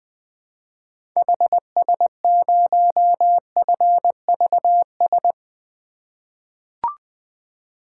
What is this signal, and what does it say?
Morse code, heard as a keyed tone, about 21 wpm, 700 Hz. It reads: HS0FVS
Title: 2023 POTA Activator Call Signs - 20wpm